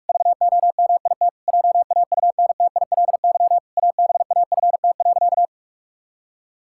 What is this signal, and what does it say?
Morse code, heard as a keyed tone, about 46 wpm, 700 Hz. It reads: VOMIT JAUNTILY ABAFT.
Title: Pangram Challenge v2 - 45wpm